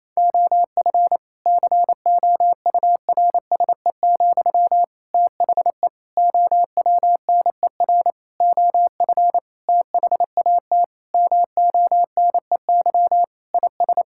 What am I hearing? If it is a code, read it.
Morse code, heard as a keyed tone, about 28 wpm, 700 Hz. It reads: OF COURSE, THE OWNER OF THAT MONEY IS
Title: Sentences from Top 1600 Words 28wpm